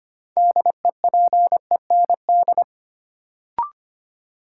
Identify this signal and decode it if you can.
Morse code, heard as a keyed tone, about 25 wpm, 700 Hz. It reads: DEPEND
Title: Sentences from Top 1300 Words - New Word Drill Alt - 25wpm